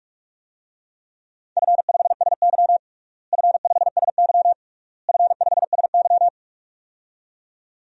Morse code, heard as a keyed tone, about 42 wpm, 700 Hz. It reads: F5SY F5SY F5SY
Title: DXCC-100 Call Signs - 4 Characters - Repeated 3x - 45wpm